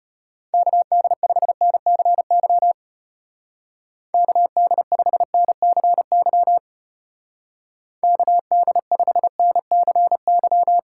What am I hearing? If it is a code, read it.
Morse code, heard as a keyed tone, about 30 wpm, 700 Hz. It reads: KD5NCY KD5NCY KD5NCY
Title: Speed Racing - USA Call Signs 25wpm